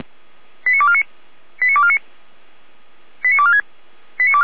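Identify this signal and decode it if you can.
Five-tone selective call (ZVEI), heard as quick runs of five tones, tones of about 70 ms. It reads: 89249 89249 89246
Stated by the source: ZVEI five-tone selective call 89163, received by SDR.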